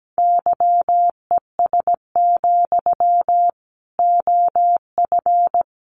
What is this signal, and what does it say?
Morse code, heard as a keyed tone, about 17 wpm, 700 Hz. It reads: YES, OF
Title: Sentences from Top 500 Words 17wpm